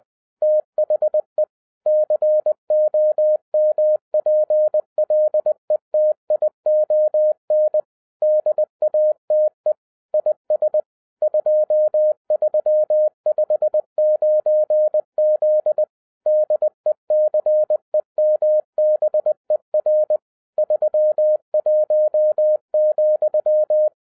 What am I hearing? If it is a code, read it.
Morse code, heard as a keyed tone, about 20 wpm, 610 Hz. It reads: THE COMPLETION DATE IS 2359Z DECEMBER 31,
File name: 2012-01-mp3 / n15-INVESTIGATION INTO THE POSSIBILITY OF WINNING PREFIX VERSION 3 AWARD-20wpm